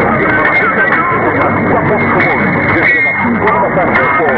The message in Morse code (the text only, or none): none